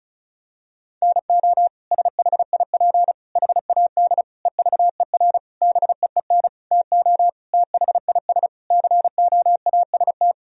Morse code, heard as a keyed tone, about 36 wpm, 700 Hz. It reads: NO SHIP HAD EVER BEEN TO THIS COAST
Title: Sentences from Top 1000 Words - Review Part 4 - 35wpm